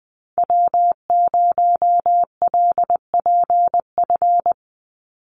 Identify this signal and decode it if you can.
Morse code, heard as a keyed tone, about 20 wpm, 700 Hz. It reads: W0LPF
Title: Call Signs Repeated 3x - 20wpm